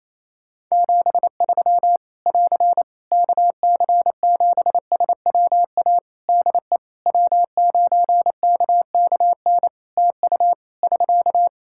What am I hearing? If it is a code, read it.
Morse code, heard as a keyed tone, about 27 wpm, 700 Hz. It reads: 73 <AR> KC7SWA DE W9KKN TU <SK>
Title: Practice QSO - Exchange 4 - Ending - 28wpm